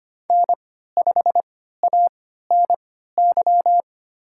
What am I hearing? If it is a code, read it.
Morse code, heard as a keyed tone, about 25 wpm, spaced as for 14 wpm, 700 Hz. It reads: N5ANY
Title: USA Call Signs - 3x Character Spacing - 25wpm